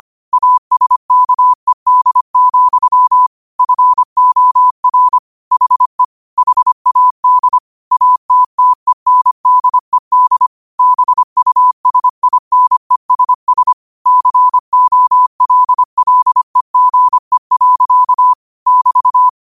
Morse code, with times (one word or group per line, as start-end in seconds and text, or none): 0.3-3.3 ASKED,
3.6-5.2 FOR
5.5-6.0 HE
6.4-7.6 HAD
7.9-10.5 ATTENDED
10.8-13.7 BUSINESS
14.1-18.3 COLLEGE.
18.7-19.3 <BT>